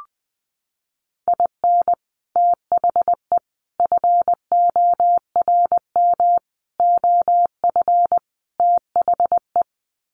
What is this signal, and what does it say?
Morse code, heard as a keyed tone, about 20 wpm, 700 Hz. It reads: IN THE FORM OF THE
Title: Sets of 5 Words Encore2 20wpm